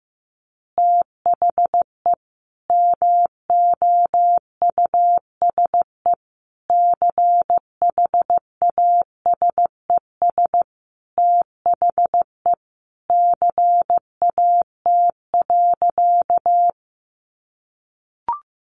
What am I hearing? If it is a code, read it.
Morse code, heard as a keyed tone, about 15 wpm, 700 Hz. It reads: THE MOUSE CHASES THE CAT.
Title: Easy Phrases 15wpm